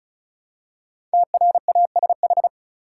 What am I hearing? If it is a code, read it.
Morse code, heard as a keyed tone, about 34 wpm, 700 Hz. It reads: TRASH
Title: Five Letter Words - 35wpm